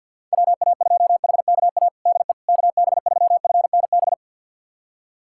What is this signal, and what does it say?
Morse code, heard as a keyed tone, about 53 wpm, 700 Hz. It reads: WA2HKA DE KB3FNB
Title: USA Callsign DE Callsign - 6 Characters - 50wpm